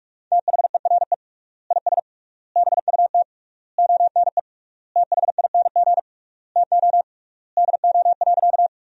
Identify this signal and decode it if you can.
Morse code, heard as a keyed tone, about 46 wpm, 700 Hz. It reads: THERE IS BUT ONE THING TO DO.
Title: 3x Word-Spacing - Sentences from Top 100 Words - 45wpm